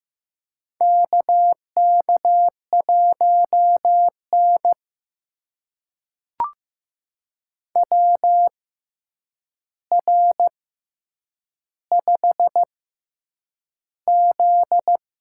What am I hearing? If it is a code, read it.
Morse code, heard as a keyed tone, about 15 wpm, spaced as for 8 wpm, 700 Hz. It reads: KK1N WR5Z
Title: USA Call Signs - 4 Characters - 6x Character Spacing - 15wpm